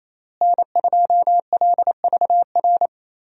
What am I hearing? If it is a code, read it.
Morse code, heard as a keyed tone, about 27 wpm, 700 Hz. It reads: N2LVR
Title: USA Call Signs - 5 Characters - Repeated 3x - 28wpm